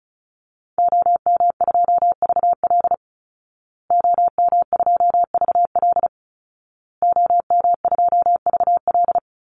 Morse code, heard as a keyed tone, about 34 wpm, 700 Hz. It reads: OM2VL OM2VL OM2VL
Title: Call Signs Repeated 2x - 35wpm